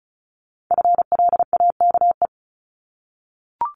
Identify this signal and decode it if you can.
Morse code, heard as a keyed tone, about 36 wpm, 700 Hz. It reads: FLAKE
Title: Multi-letter Phonemes 35wpm